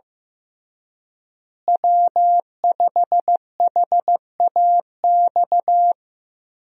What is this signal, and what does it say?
Morse code, heard as a keyed tone, about 15 wpm, 700 Hz. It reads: W5HAX
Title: USA Call Signs - 5 Characters - Repeated 3x - 15wpm